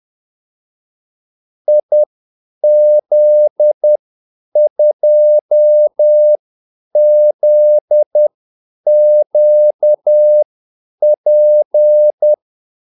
Morse code, heard as a keyed tone, about 10 wpm, spaced as for 8 wpm, 600 Hz. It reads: IZ2ZQP
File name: Callsigns-10-8wpm